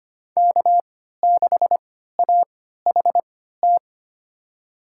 Morse code, heard as a keyed tone, about 25 wpm, spaced as for 14 wpm, 700 Hz. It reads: K6AHT
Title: USA Call Signs - 5 Characters - 3x Character Spacing - 25wpm